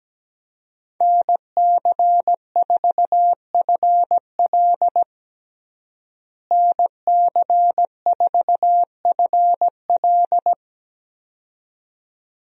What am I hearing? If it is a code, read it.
Morse code, heard as a keyed tone, about 17 wpm, 700 Hz. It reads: NC4FL NC4FL
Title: USA Call Signs - 5 Characters - Repeated 2x - 17wpm